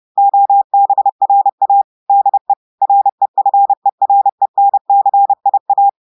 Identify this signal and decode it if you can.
Morse code, heard as a keyed tone, about 30 wpm, 800 Hz. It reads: OBRA DE REFERENCIA
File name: audios / portugais / portugais texte en clair 1 vitesse 30 mots minutes